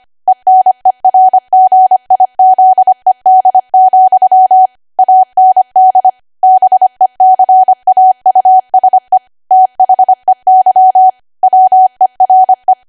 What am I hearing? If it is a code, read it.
Morse code, heard as a keyed tone, about 25 wpm, 750 Hz. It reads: ENERGIZED, AND BECAUSE THEY WERE
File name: cwa / W1AW-Level3 / 25WPM